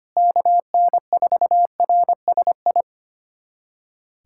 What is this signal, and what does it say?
Morse code, heard as a keyed tone, about 25 wpm, 700 Hz. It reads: KN4RSI
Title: USA Callsign DE Callsign - 6 Characters - 25wpm